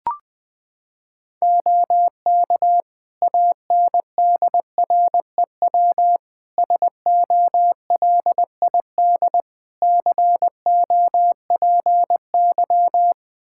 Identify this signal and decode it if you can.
Morse code, heard as a keyed tone, about 20 wpm, 700 Hz. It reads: OK ANDREW SOLID COPY
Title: Practice QSO - Exchange 3 - More Info - 20wpm